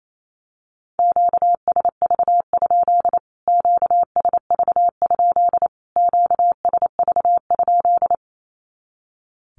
Morse code, heard as a keyed tone, about 28 wpm, 700 Hz. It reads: QSV? QSV? QSV?
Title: Q-Codes Repeated 3x 28wpm